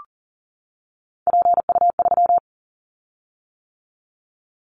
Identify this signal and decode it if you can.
Morse code, heard as a keyed tone, about 42 wpm, 700 Hz. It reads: PU3
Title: Callsign Prefixes for Top 100 DXCC - 40wpm